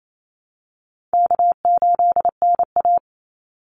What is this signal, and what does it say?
Morse code, heard as a keyed tone, about 28 wpm, 700 Hz. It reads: K8NA
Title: Easy Call Signs - 28wpm